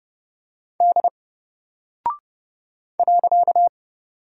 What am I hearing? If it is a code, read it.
Morse code, heard as a keyed tone, about 29 wpm, 700 Hz. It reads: D .
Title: Single Character (Letter:Number:Punctuation:Prosign) 30wpm